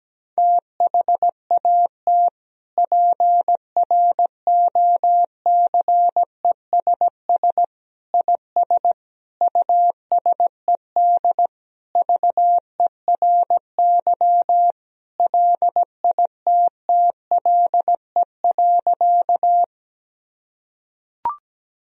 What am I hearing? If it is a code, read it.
Morse code, heard as a keyed tone, about 17 wpm, 700 Hz. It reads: THAT PROCESS IS USED VERY LITTLE.
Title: Sentences from Top 1000 Words - Review Part 3 - 17wpm